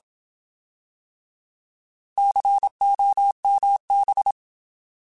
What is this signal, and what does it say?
Morse code, heard as a keyed tone, about 27 wpm, 800 Hz. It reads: COMB
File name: Alphabetical